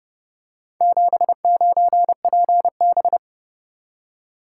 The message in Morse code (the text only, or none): 79PB